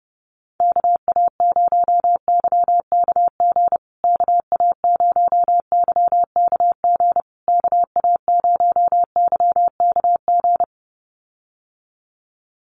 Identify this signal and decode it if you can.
Morse code, heard as a keyed tone, about 30 wpm, 700 Hz. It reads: KA0YKG KA0YKG KA0YKG
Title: USA Call Signs Repeated 3x 30wpm